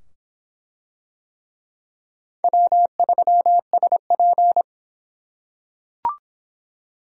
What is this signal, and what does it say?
Morse code, heard as a keyed tone, about 25 wpm, 700 Hz. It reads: W3SP